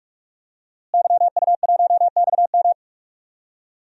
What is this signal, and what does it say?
Morse code, heard as a keyed tone, about 41 wpm, 700 Hz. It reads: YU1XM
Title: DXCC-100 Call Signs - 5 Characters - Repeated 3x - 45wpm